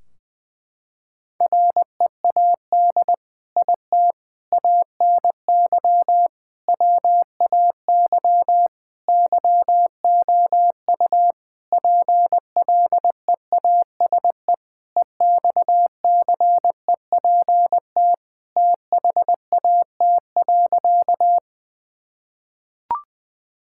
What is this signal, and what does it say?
Morse code, heard as a keyed tone, about 20 wpm, 700 Hz. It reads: READ IT ANY WAY YOU PLEASE EXCEPT THAT.
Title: Sentences from Top 1600 Words 20wpm